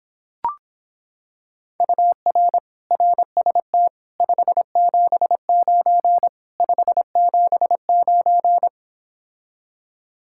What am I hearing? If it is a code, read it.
Morse code, heard as a keyed tone, about 26 wpm, 700 Hz. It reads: UR RST 579 579